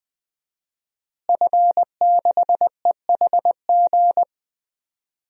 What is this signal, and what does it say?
Morse code, heard as a keyed tone, about 20 wpm, 700 Hz. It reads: F6EHG
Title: DXCC-100 Call Signs - 5 Characters - Repeated 3x - 20wpm